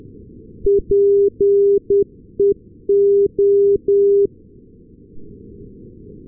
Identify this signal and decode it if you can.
Morse code, heard as a keyed tone, about 10 wpm, 400 Hz. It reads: PEO
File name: europe / s-280 ndb nykoping-peola peo-398, sm1otx